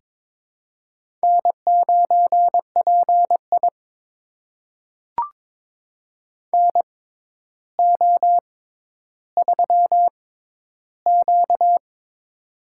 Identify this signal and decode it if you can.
Morse code, heard as a keyed tone, about 22 wpm, spaced as for 11 wpm, 700 Hz. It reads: N9PI NO3Q
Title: USA Call Signs - 6x Character Spacing - 22wpm